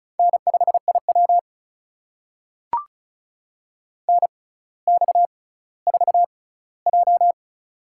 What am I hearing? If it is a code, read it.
Morse code, heard as a keyed tone, about 36 wpm, spaced as for 18 wpm, 700 Hz. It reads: N5IW NX4J
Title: USA Call Signs - 4 Characters - 6x Character Spacing - 35wpm